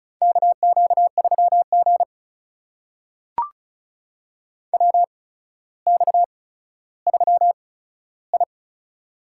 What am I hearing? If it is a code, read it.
Morse code, heard as a keyed tone, about 36 wpm, spaced as for 15 wpm, 700 Hz. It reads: KQ3G WX3I
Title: USA Call Signs - 4 Characters - 8x Character Spacing - 35wpm